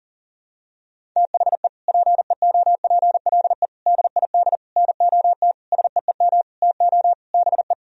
Morse code, heard as a keyed tone, about 40 wpm, 700 Hz. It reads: THE PEOPLE DID NOT SEEM TO BE
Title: Sentences from Top 1000 Words - Review Part 1 - 40wpm